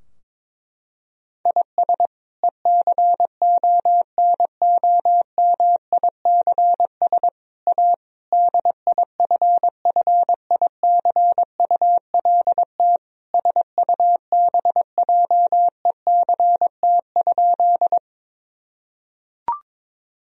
Morse code, heard as a keyed tone, about 22 wpm, 700 Hz. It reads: IS ECONOMICS A DIFFICULT SUBJECT?
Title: Sentences from Top 1000 Words - Review Part 3 - 22wpm